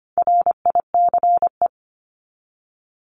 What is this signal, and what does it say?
Morse code, heard as a keyed tone, about 25 wpm, 700 Hz. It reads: RICE
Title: Top 1501-1750 Words 25wpm